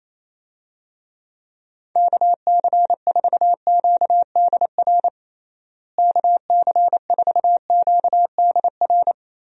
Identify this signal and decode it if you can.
Morse code, heard as a keyed tone, about 28 wpm, 700 Hz. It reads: KC4QDR KC4QDR
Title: USA Call Signs - 6 Characters - Repeated 2x - 28wpm